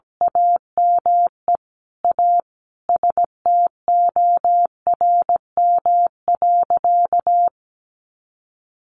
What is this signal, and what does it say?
Morse code, heard as a keyed tone, about 17 wpm, 700 Hz. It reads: AME A STORM.
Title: Sentences from Top 1500 Words 17wpm